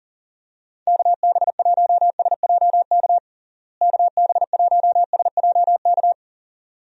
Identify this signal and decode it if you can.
Morse code, heard as a keyed tone, about 37 wpm, 700 Hz. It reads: KB1SJK KB1SJK
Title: USA Call Signs - 6 Characters - Repeated 2x - 40wpm